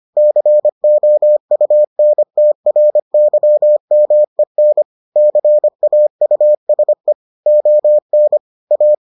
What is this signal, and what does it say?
Morse code, heard as a keyed tone, about 25 wpm, 600 Hz. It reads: COUNTRYMEN CAUSE ON A